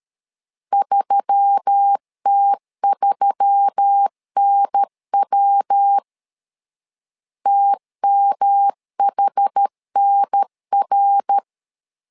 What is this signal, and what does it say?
Morse code, heard as a keyed tone, about 13 wpm, 800 Hz. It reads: 3T3NW TMHNR
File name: LEZIONI / 4 Q1S3 12wpm 2023